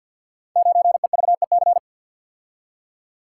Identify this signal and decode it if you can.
Morse code, heard as a keyed tone, about 50 wpm, 700 Hz. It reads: 9EVEC
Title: Code Groups - 5 Letter:Numbers - 50wpm